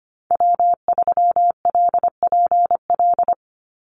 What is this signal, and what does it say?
Morse code, heard as a keyed tone, about 25 wpm, 700 Hz. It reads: W3LPL
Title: Call Signs Repeated 3x - 25wpm